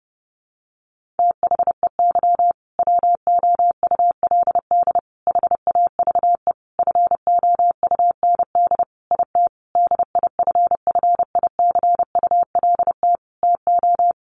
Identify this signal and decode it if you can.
Morse code, heard as a keyed tone, about 30 wpm, 700 Hz. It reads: THEY WOULD HAVE FOUND IT DIFFICULT TO
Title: Sentences from Top 1400 Words 30wpm